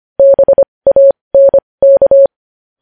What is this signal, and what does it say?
Morse code, heard as a keyed tone, about 25 wpm, spaced as for 20 wpm, 550 Hz. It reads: BANK
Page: news-headlines-in-morse-code-at-20-wpm